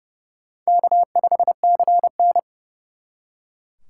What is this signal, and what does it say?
Morse code, heard as a keyed tone, about 37 wpm, 700 Hz. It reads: K5CN
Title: Speed Racing - USA 1x2 Call Signs 30wpm